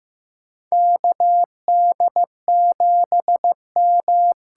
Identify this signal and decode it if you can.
Morse code, heard as a keyed tone, about 15 wpm, 700 Hz. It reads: KD7M
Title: USA Call Signs - 6 Characters - 8x Character Spacing - 15wpm